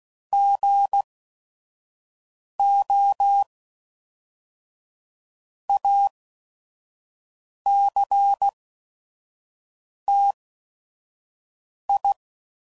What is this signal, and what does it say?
Morse code, heard as a keyed tone, about 16 wpm, spaced as for 5 wpm, 800 Hz. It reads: GO ACTI
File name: CW Code Groups @5WPM